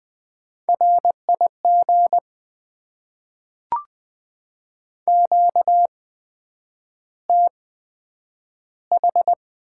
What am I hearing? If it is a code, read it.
Morse code, heard as a keyed tone, about 20 wpm, spaced as for 9 wpm, 700 Hz. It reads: RIG QTH
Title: Top 20 QSO Elements - 8x Character Spacing - 20wpm